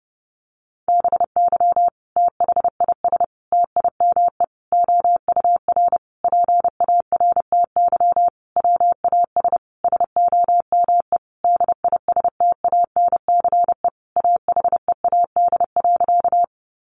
Morse code, heard as a keyed tone, about 30 wpm, 700 Hz. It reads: BY THIS TIME OUR PARTY WAS SOME DISTANCE AHEAD.
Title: Sentences from Top 1400 Words 30wpm